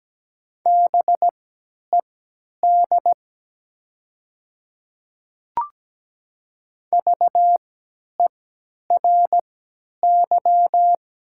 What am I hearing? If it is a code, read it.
Morse code, heard as a keyed tone, about 17 wpm, spaced as for 10 wpm, 700 Hz. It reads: BED VERY
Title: 3x Character Spacing - Top 500 Words - 17wpm